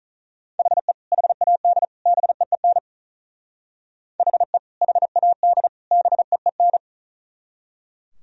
Morse code, heard as a keyed tone, about 44 wpm, 700 Hz. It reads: HE HAD BEEN HE HAD BEEN